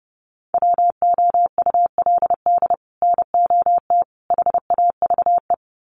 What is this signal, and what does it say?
Morse code, heard as a keyed tone, about 30 wpm, 700 Hz. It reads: WOULD NOT HAVE
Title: Sets of 3 Words 30wpm